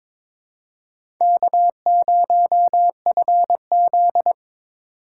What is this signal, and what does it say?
Morse code, heard as a keyed tone, about 22 wpm, 700 Hz. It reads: K0FZ
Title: USA Call Signs - 4 Characters - 3x Character Spacing - 22wpm